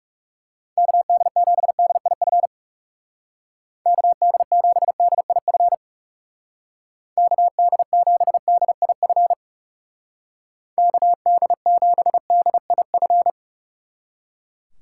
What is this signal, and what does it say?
Morse code, heard as a keyed tone, about 36 wpm, 700 Hz. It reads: KD7DIF KD7DIF KD7DIF KD7DIF
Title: Speed Racing - USA 2x3 Call Signs 30wpm